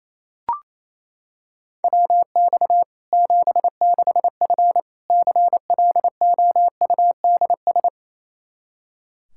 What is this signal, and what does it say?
Morse code, heard as a keyed tone, about 28 wpm, 700 Hz. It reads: WX 76F CLOUDS
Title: Practice QSO - Exchange 2 - Info - 28wpm